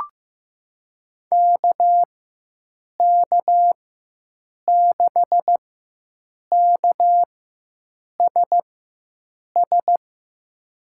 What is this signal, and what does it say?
Morse code, heard as a keyed tone, about 16 wpm, spaced as for 7 wpm, 700 Hz. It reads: KK6KSS
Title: USA Call Signs - 6 Characters - 4x Character Spacing - 15wpm